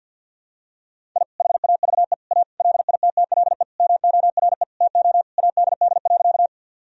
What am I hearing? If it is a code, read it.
Morse code, heard as a keyed tone, about 51 wpm, 700 Hz. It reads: I HAVE A LITTLE MORE TO ADD.
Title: Sentences from Top 1000 Words - Review Part 1 - 50wpm